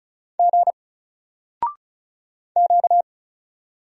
Morse code, heard as a keyed tone, about 36 wpm, 700 Hz. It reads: G Q
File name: Single-Letter-Number-35wpm